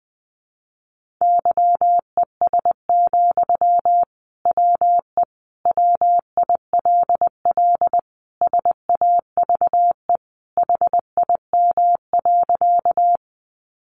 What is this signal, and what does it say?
Morse code, heard as a keyed tone, about 20 wpm, 700 Hz. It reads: YES, WE WILL SAVE HIM.
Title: Sentences from Top 1100 Words 20wpm